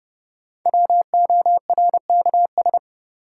Morse code, heard as a keyed tone, about 30 wpm, 700 Hz. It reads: WORKS
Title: Sentences from Top 1000 Words - Review Part 5 - New Word Drill - 30wpm